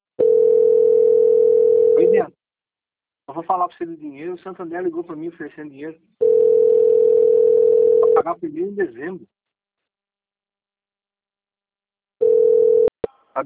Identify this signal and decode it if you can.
Ringback tone (North America), 440 + 480 Hz, with speech.
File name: out- - - - - .9184